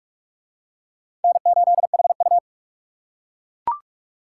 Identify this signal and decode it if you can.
Morse code, heard as a keyed tone, about 38 wpm, 700 Hz. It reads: N8HU